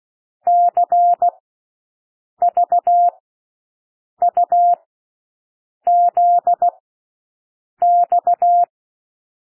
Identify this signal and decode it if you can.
Morse code, heard as a keyed tone, about 16 wpm, spaced as for 6 wpm, 700 Hz. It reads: CVUZX
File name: cwmp3 / L1 SOUND LETTERS 16 wpm